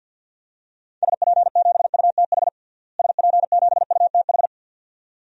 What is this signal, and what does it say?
Morse code, heard as a keyed tone, about 49 wpm, 700 Hz. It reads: SP7UTH SP7UTH
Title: DXCC-100 Call Signs - 6 Characters - Repeated 2x - 50wpm